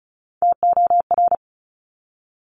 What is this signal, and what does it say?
Morse code, heard as a keyed tone, about 36 wpm, 700 Hz. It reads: TOR
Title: Top Three Letter Combinations in Words 35wpm